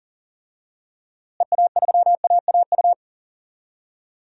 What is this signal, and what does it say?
Morse code, heard as a keyed tone, about 40 wpm, 700 Hz. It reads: EA3AAU
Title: DXCC-100 Call Signs - 6 Characters - Repeated 3x - 40wpm